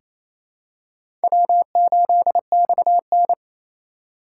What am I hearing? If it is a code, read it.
Morse code, heard as a keyed tone, about 28 wpm, 700 Hz. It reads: W8XN
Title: USA Call Signs - 4 Characters - 3x Character Spacing - 28wpm